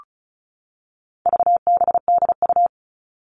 Morse code, heard as a keyed tone, about 34 wpm, 700 Hz. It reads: VBDU
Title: All Cut Numbers - 4-Digit 35wpm